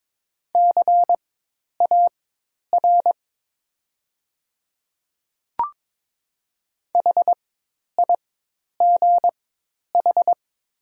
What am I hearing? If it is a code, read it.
Morse code, heard as a keyed tone, about 22 wpm, spaced as for 10 wpm, 700 Hz. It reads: CAR HIGH
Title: 4x Character Spacing - Top 300 Words - 22wpm